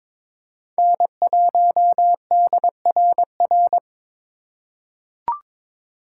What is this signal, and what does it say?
Morse code, heard as a keyed tone, about 21 wpm, 700 Hz. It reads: N1DRR
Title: USA Call Signs - 5 Characters - 2x Character Spacing - 22wpm